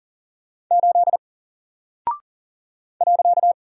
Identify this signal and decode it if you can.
Morse code, heard as a keyed tone, about 40 wpm, 700 Hz. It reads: 8 .
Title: Single Character (Letter:Number:Punctuation:Prosign) 40wpm